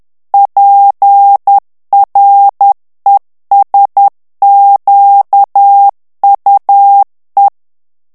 Morse code, heard as a keyed tone, about 10 wpm, 800 Hz. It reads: PRESQUE
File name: morse mp3 / FRANCAIS TEXTE CLAIR 2 VITESSE 10 MOTS MINUTES